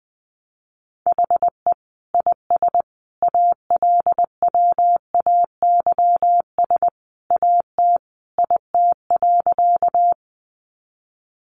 Morse code, heard as a keyed tone, about 19 wpm, 700 Hz. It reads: HE IS ALWAYS AT IT.
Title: Sentences from Top 300 Words 20wpm